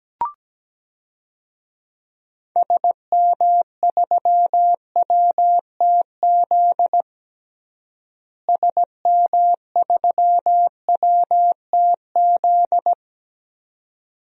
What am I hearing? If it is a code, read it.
Morse code, heard as a keyed tone, about 17 wpm, 700 Hz. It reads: SM3WTZ SM3WTZ
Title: DXCC-100 Call Signs - 6 Characters - Repeated 2x - 17wpm